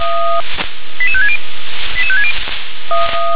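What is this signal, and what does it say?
Five-tone selective call (ZVEI), heard as quick runs of five tones, tones of about 70 ms. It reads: 88560 88560
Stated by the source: ZVEI five-tone call received by SDR.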